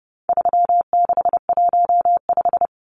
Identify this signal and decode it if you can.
Morse code, heard as a keyed tone, about 30 wpm, 700 Hz. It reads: 3615
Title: Numbers - 4-Digit 30wpm